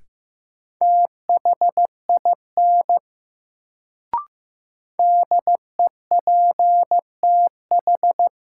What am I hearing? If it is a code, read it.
Morse code, heard as a keyed tone, about 15 wpm, 700 Hz. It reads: THIN DEPTH
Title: Sentences from Top 1200 Words - New Word Drill Alt - 15wpm